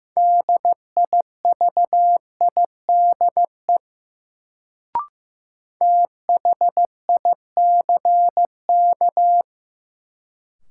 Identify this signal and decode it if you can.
Morse code, heard as a keyed tone, about 15 wpm, 700 Hz. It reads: DIVIDE THICK
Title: Sentences from Top 1600 Words - New Word Drill Alt - 15wpm